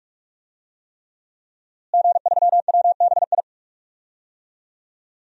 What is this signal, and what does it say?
Morse code, heard as a keyed tone, about 41 wpm, 700 Hz. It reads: G3WBI
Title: DXCC-100 Call Signs - 5 Characters - Repeated 3x - 45wpm